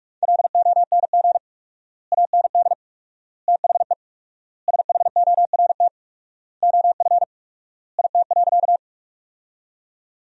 Morse code, heard as a keyed tone, about 47 wpm, 700 Hz. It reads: LONG AND THE SHORT OF IT.
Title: 4x Word-Spacing - Sentences from Top 400 Words - 45wpm